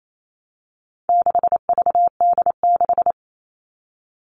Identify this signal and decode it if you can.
Morse code, heard as a keyed tone, about 28 wpm, 700 Hz. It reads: 6VD6
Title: All Cut Numbers - 4-Digit 28wpm